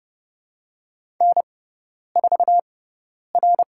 Morse code, heard as a keyed tone, about 30 wpm, 700 Hz. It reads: N 4 R
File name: Lesson 16g - Letter c - Call Signs 30wpm